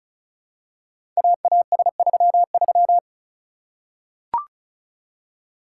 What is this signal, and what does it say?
Morse code, heard as a keyed tone, about 36 wpm, 700 Hz. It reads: AAS33
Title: Code Groups - 5 Letter:Numbers - 35wpm